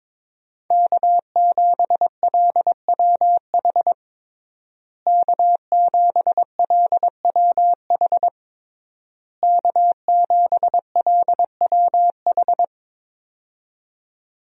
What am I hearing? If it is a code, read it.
Morse code, heard as a keyed tone, about 22 wpm, 700 Hz. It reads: K7LWH K7LWH K7LWH
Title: USA Call Signs - 5 Characters - Repeated 3x - 22wpm